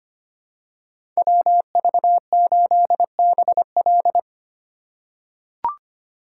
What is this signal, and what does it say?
Morse code, heard as a keyed tone, about 25 wpm, 700 Hz. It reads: WV8BL
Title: USA Call Signs - 4x Character Spacing - 25wpm